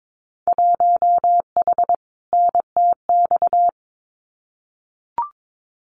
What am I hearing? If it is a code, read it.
Morse code, heard as a keyed tone, about 22 wpm, 700 Hz. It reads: 1H NTX
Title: Winter Field Day Exchanges 22wpm